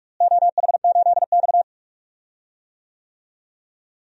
Morse code, heard as a keyed tone, about 41 wpm, 700 Hz. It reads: OH8X
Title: DXCC-100 Call Signs - 4 Characters - Repeated 2x - 45wpm